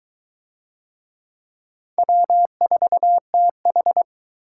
Morse code, heard as a keyed tone, about 22 wpm, 700 Hz. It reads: W4TH